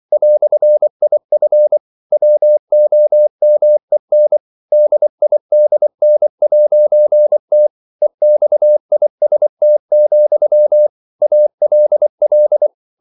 Morse code, heard as a keyed tone, about 24 wpm, 600 Hz. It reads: "IF WOMEN DIDN'T EXIST, ALL
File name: CW / Practice / EnglishQuotations0324